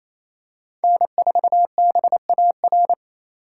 Morse code, heard as a keyed tone, about 28 wpm, 700 Hz. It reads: N4BAR
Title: USA Call Signs - 4x Character Spacing - 28wpm